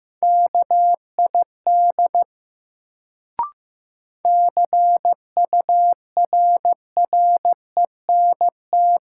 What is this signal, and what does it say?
Morse code, heard as a keyed tone, about 15 wpm, 700 Hz. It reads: KID CURRENT
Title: Sentences from Top 1600 Words - New Word Drill Alt - 15wpm